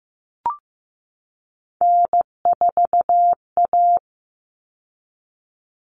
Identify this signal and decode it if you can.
Morse code, heard as a keyed tone, about 15 wpm, 700 Hz. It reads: N4A
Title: Common Cut Numbers - 3-Digit 15wpm